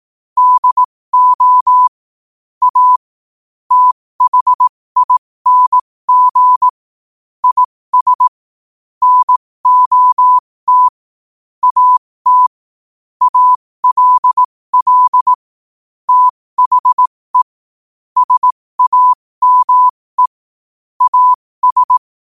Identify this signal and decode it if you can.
Morse code, heard as a keyed tone, about 18 wpm, spaced as for 15 wpm, 1000 Hz. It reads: DO A THING IS NOT AT ALL THE SAME AS
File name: qotdfiles / 15wpm